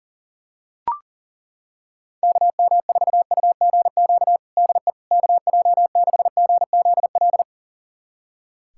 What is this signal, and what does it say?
Morse code, heard as a keyed tone, about 40 wpm, 700 Hz. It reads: KM4UGQ DE KJ6GZL
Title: USA Callsign DE Callsign 40wpm